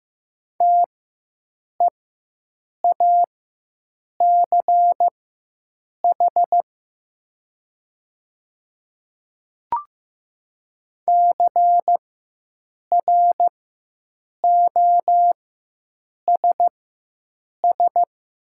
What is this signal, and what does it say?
Morse code, heard as a keyed tone, about 15 wpm, spaced as for 7 wpm, 700 Hz. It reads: TEACH CROSS
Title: 4x Character Spacing - Top 500 Words - 15wpm